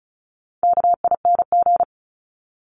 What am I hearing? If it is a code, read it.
Morse code, heard as a keyed tone, about 34 wpm, 700 Hz. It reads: KING
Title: Top 500 Words 35wpm